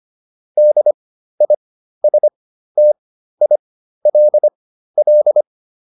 Morse code, heard as a keyed tone, about 25 wpm, spaced as for 13 wpm, 600 Hz. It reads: DISTILL